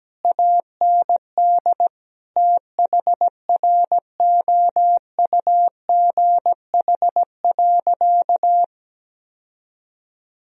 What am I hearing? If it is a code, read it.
Morse code, heard as a keyed tone, about 17 wpm, 700 Hz. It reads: AND THROUGH.
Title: Sentences from Top 1000 Words - Review Part 1 - 17wpm